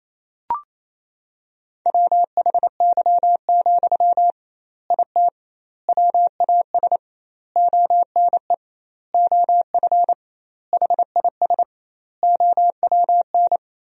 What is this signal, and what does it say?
Morse code, heard as a keyed tone, about 28 wpm, 700 Hz. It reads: WHY, IT WAS ONE OF HIS OWN
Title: 2x Word-Spacing - Sentences from Top 300 Words - 28wpm